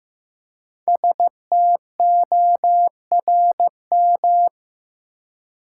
Morse code, heard as a keyed tone, about 15 wpm, 700 Hz. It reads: STORM
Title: Five Letter Words - 15wpm